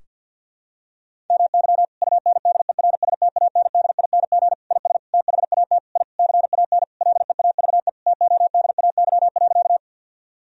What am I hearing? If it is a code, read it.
Morse code, heard as a keyed tone, about 52 wpm, 700 Hz. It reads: MY UNDERSTANDING IS THAT I CAN LEAVE TODAY.
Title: Sentences from Top 1000 Words - Review Part 3 - 50wpm